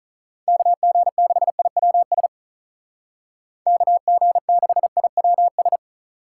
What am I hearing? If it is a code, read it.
Morse code, heard as a keyed tone, about 46 wpm, 700 Hz. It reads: KG6IWS KG6IWS